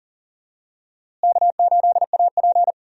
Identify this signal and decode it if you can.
Morse code, heard as a keyed tone, about 40 wpm, 700 Hz. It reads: K8AP
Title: USA Call Signs - 4 Characters - Repeated 2x - 40wpm